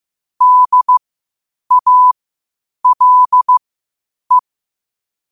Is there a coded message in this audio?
Yes: Morse code from 0.4 to 4.4 s, heard as a keyed tone, about 15 wpm, spaced as for 8 wpm, 1000 Hz. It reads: DALE